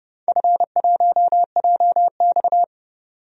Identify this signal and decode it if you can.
Morse code, heard as a keyed tone, about 30 wpm, 700 Hz. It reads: F1JX
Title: DXCC-100 Call Signs - 4 Characters - 30wpm